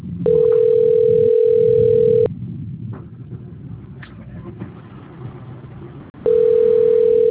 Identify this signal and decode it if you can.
Ringback tone (North America), 440 + 480 Hz.